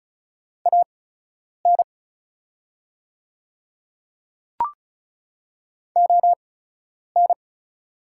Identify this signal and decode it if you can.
Morse code, heard as a keyed tone, about 36 wpm, spaced as for 10 wpm, 690 Hz. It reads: AN ON
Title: 8x Character Spacing - Top 500 Words - 35wpm